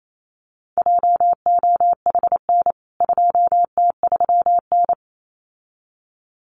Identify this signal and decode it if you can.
Morse code, heard as a keyed tone, about 28 wpm, 700 Hz. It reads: JOHN 2T3N
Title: CWT Exchange 28wpm